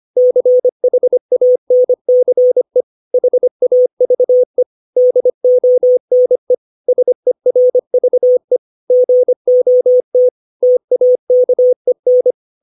Morse code, heard as a keyed tone, about 25 wpm, 500 Hz. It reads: CHANCE HAVE DONE SERVE GOT TAKEN
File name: 632 words / 632 Words 25WPM